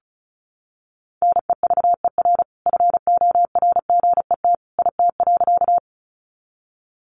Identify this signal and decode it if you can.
Morse code, heard as a keyed tone, about 34 wpm, 700 Hz. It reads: NEVER FORGET IT.
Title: Sentences from Top 1200 Words 35wpm